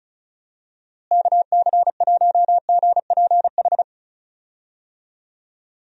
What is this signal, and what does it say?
Morse code, heard as a keyed tone, about 34 wpm, 700 Hz. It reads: KC1GPH
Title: USA Call Signs - 6 Characters - Repeated 3x - 35wpm